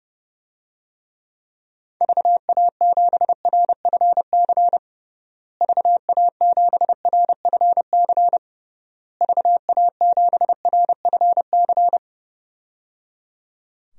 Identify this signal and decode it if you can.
Morse code, heard as a keyed tone, about 30 wpm, 700 Hz. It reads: VA7RFC VA7RFC VA7RFC
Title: DXCC-100 Call Signs - 6 Characters - Repeated 3x - 30wpm